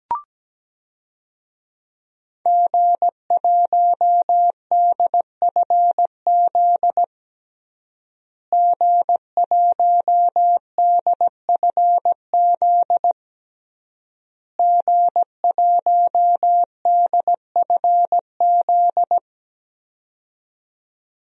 Morse code, heard as a keyed tone, about 17 wpm, 700 Hz. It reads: G1DFZ G1DFZ G1DFZ
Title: DXCC-100 Call Signs - 5 Characters - Repeated 3x - 17wpm